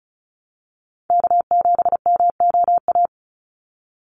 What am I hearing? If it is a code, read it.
Morse code, heard as a keyed tone, about 34 wpm, 700 Hz. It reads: K7MOA
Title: Call Signs Repeated 2x - 35wpm